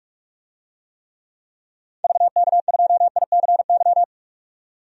Morse code, heard as a keyed tone, about 44 wpm, 690 Hz. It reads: VK2ICY
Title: DXCC-100 Call Signs - 6 Characters - 45wpm